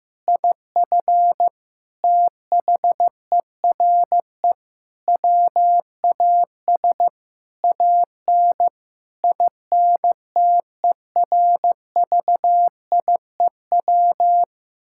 Morse code, heard as a keyed tone, about 16 wpm, 700 Hz. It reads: IF THERE WAS AN INTERVIEW
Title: Sentences from Top 1600 Words 15wpm